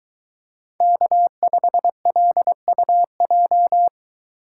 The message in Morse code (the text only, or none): K5LUJ